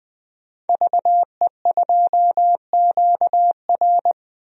Morse code, heard as a keyed tone, about 21 wpm, 700 Hz. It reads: VE2QR
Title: DXCC-100 Call Signs - 5 Characters - 20wpm